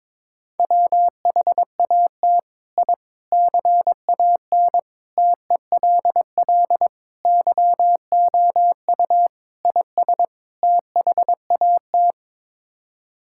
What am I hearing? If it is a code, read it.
Morse code, heard as a keyed tone, about 22 wpm, 700 Hz. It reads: WHAT I CAN TELL YOU IS THAT
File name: practice / 22wpm / Sets of 7 Words - 22wpm